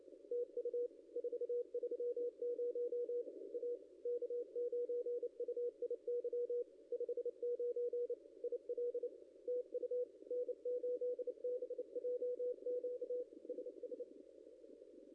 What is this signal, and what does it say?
Morse code, heard as a keyed tone, about 24 wpm, 500 Hz. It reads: TU 430 A K9UIY 59 IL TU N8BJQ SS
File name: contests / ss-cw-k5tr-op-n5rz -0600